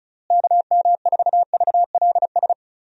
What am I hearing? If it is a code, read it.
Morse code, heard as a keyed tone, about 36 wpm, 700 Hz. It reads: KM4VLS
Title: USA Call Signs - 6 Characters - Repeated 3x - 35wpm